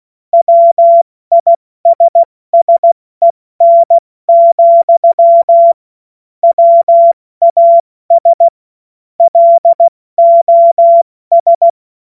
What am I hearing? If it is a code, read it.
Morse code, heard as a keyed tone, about 16 wpm, spaced as for 14 wpm, 680 Hz. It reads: WISSEN, WAS LOS
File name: Morsekurs / Witze / Witz 27 Mai 2025 70 F80